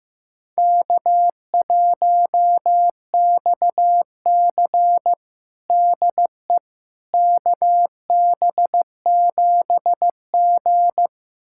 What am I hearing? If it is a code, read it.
Morse code, heard as a keyed tone, about 15 wpm, 700 Hz. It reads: K1XC DE KB7G
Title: USA Callsign DE Callsign - 4 Characters - 15wpm